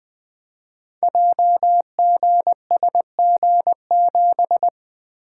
Morse code, heard as a keyed tone, about 20 wpm, 700 Hz. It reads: JGSG7
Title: Code Groups - 5 Letter:Numbers - 20wpm